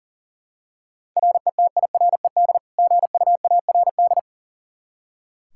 Morse code, heard as a keyed tone, about 40 wpm, 700 Hz. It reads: RETIRED GUARD
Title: Practice QSO - Exchange 3 - More Info - 40wpm